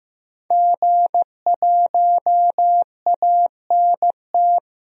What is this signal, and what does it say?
Morse code, heard as a keyed tone, about 15 wpm, 700 Hz. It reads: G1ANT
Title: DXCC-100 Call Signs - 5 Characters - 15wpm